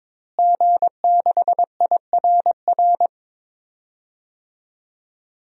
Morse code, heard as a keyed tone, about 22 wpm, 700 Hz. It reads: G6IRR
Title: DXCC-100 Call Signs - 5 Characters - Repeated 3x - 22wpm